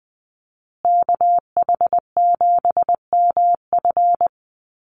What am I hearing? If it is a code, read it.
Morse code, heard as a keyed tone, about 20 wpm, 700 Hz. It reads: KH7MF
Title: USA Call Signs Repeated 2x 20wpm